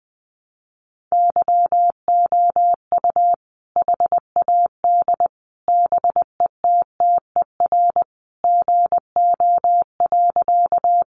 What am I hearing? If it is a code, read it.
Morse code, heard as a keyed tone, about 20 wpm, 700 Hz. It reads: YOU HAD BETTER GO.
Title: Sentences from Top 400 Words 20wpm